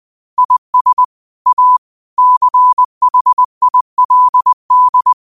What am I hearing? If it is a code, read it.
Morse code, heard as a keyed tone, about 20 wpm, 1000 Hz. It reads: IS A CHILD